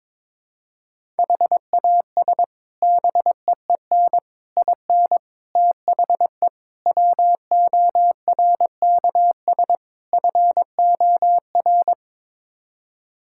Morse code, heard as a keyed tone, about 22 wpm, 700 Hz. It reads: HAS BEEN IN THE WORKS FOR
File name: practice / 22wpm / Sets of 6 Words - 22wpm